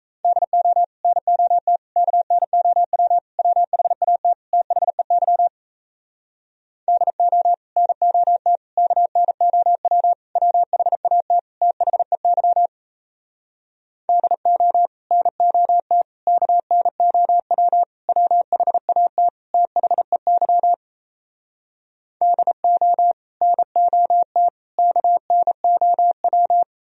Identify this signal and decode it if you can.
Morse code, heard as a keyed tone, about 37 wpm, 700 Hz. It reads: DO NOT KNOW WHAT THEY DO NOT KNOW WHAT THEY DO NOT KNOW WHAT THEY DO NOT KNOW